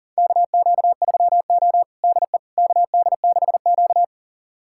Morse code, heard as a keyed tone, about 40 wpm, 700 Hz. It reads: KQ3O DE KD6Q
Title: USA Callsign DE Callsign - 4 Characters - 40wpm